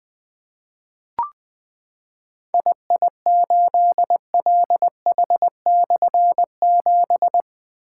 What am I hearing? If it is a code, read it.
Morse code, heard as a keyed tone, about 20 wpm, 700 Hz. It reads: II8LH/7
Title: Difficult Call Signs 20wpm